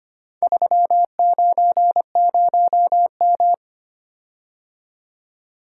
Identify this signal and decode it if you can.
Morse code, heard as a keyed tone, about 25 wpm, 700 Hz. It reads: 390M
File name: practice / 25wpm / Four Letter-Number Combinations - 25wpm